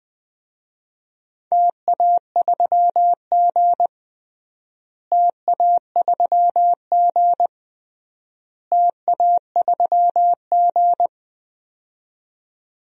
Morse code, heard as a keyed tone, about 21 wpm, 700 Hz. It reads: TA3G TA3G TA3G
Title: DXCC-100 Call Signs - 4 Characters - Repeated 3x - 20wpm